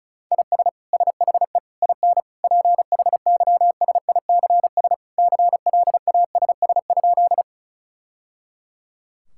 Morse code, heard as a keyed tone, about 36 wpm, 700 Hz. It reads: IS SHE IN PHYSICS CLASS?
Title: Sentences from Top 1000 Words - Review Part 5 - 35wpm